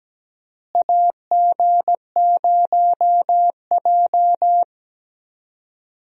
Morse code, heard as a keyed tone, about 17 wpm, 700 Hz. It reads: AG0J
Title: USA Call Signs - 4 Characters - Repeated 2x - 17wpm